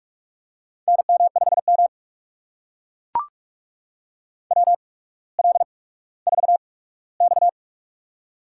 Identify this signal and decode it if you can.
Morse code, heard as a keyed tone, about 44 wpm, spaced as for 19 wpm, 690 Hz. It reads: NM5M WL4X
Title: USA Call Signs - 4 Characters - 8x Character Spacing - 45wpm